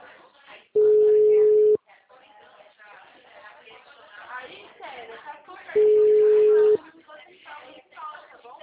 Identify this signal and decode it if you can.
Ringback tone (Europe, 425 Hz), with speech.
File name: out- -2019- - - .20524